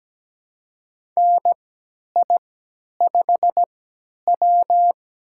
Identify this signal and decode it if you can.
Morse code, heard as a keyed tone, about 17 wpm, spaced as for 10 wpm, 700 Hz. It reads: NI5W
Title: USA Call Signs - 4 Characters - 3x Character Spacing - 17wpm